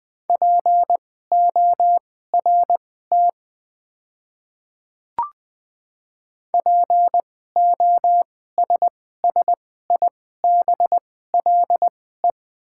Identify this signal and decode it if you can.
Morse code, heard as a keyed tone, about 20 wpm, spaced as for 14 wpm, 690 Hz. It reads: PORT POSSIBLE
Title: 2x Character Spacing - Top 500 Words - 20wpm